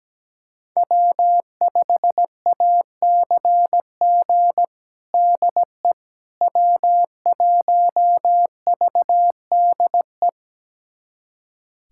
Morse code, heard as a keyed tone, about 17 wpm, 700 Hz. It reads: W5ACG DE W1VDE
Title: USA Callsign DE Callsign - 5 Characters - 17wpm